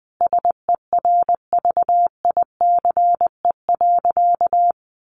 Morse code, heard as a keyed tone, about 20 wpm, 700 Hz. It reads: SERVICE.
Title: Sentences from Top 1400 Words 20wpm